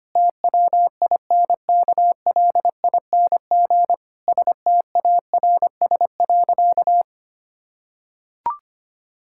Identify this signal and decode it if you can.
Morse code, heard as a keyed tone, about 25 wpm, 700 Hz. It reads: TWINKLING STARS.
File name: practice / 25wpm / Difficult Sentences with Hints - 25wpm